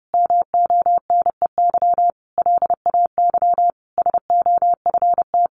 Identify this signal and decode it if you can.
Morse code, heard as a keyed tone, about 30 wpm, 700 Hz. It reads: MONEY LAY SOFT
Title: Sentences from Top 1500 Words 30wpm